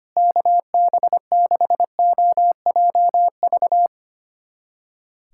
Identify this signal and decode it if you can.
Morse code, heard as a keyed tone, about 25 wpm, 700 Hz. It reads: KB6OJV
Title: USA Callsign DE Callsign - 6 Characters - 25wpm